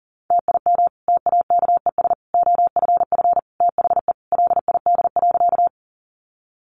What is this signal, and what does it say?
Morse code, heard as a keyed tone, about 40 wpm, 700 Hz. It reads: TIM TAKES OFF THE LID.
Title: Easy Phrases 40wpm